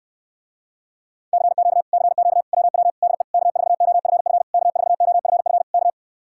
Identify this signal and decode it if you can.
Morse code, heard as a keyed tone, about 75 wpm, 700 Hz. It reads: CQ CQ WW DE K4OFF K4OFF K